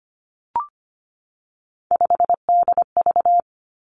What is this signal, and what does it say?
Morse code, heard as a keyed tone, about 25 wpm, 700 Hz. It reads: 5DV
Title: Three Challenging Letter-Number Combinations 25wpm